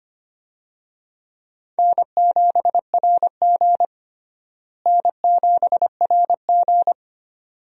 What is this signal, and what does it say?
Morse code, heard as a keyed tone, about 25 wpm, 700 Hz. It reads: N7RG N7RG
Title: USA Call Signs - 4 Characters - Repeated 2x - 25wpm